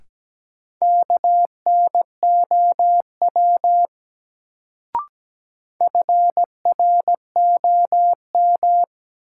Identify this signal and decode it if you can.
Morse code, heard as a keyed tone, about 17 wpm, 700 Hz. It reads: KNOW FROM
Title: Sentences from Top 100 Words - New Word Drill - 17wpm